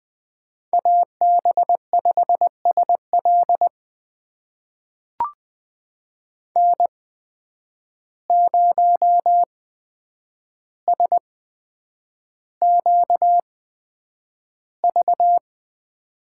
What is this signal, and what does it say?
Morse code, heard as a keyed tone, about 20 wpm, spaced as for 9 wpm, 700 Hz. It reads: AB5SL N0SQV
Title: USA Call Signs - 5 Characters - 8x Character Spacing - 20wpm